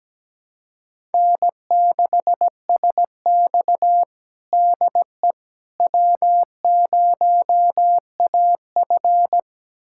Morse code, heard as a keyed tone, about 17 wpm, 700 Hz. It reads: N6SX DE W0AF
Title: USA Callsign DE Callsign - 4 Characters - 17wpm